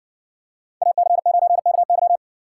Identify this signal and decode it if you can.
Morse code, heard as a keyed tone, about 56 wpm, 700 Hz. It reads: AC9KY